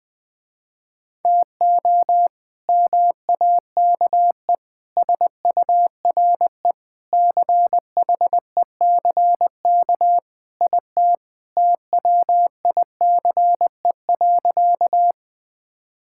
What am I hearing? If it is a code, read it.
Morse code, heard as a keyed tone, about 20 wpm, 700 Hz. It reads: TO MAKE SURE CHECK IT TWICE.
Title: Sentences with 3-Word Phrases - Spoken Beginning - 20wpm